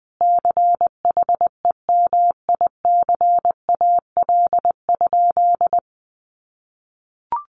Morse code, heard as a keyed tone, about 20 wpm, 700 Hz. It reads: CHEMICAL?
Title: Sentences from Top 1200 Words 20wpm